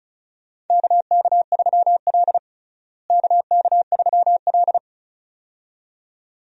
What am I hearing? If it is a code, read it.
Morse code, heard as a keyed tone, about 34 wpm, 700 Hz. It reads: KK3L KK3L
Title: USA Call Signs - 4 Characters - Repeated 2x - 35wpm